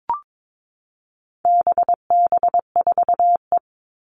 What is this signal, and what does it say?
Morse code, heard as a keyed tone, about 22 wpm, 700 Hz. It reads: BB4E
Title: All Cut Numbers - 4-Digit 22wpm